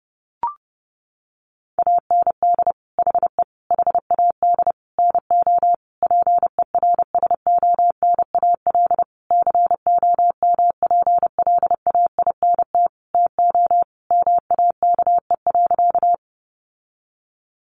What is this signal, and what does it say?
Morse code, heard as a keyed tone, about 30 wpm, 700 Hz. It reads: AND HE HAD NO PERSONAL COMPLAINT TO MAKE.
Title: Sentences from Top 1400 Words 30wpm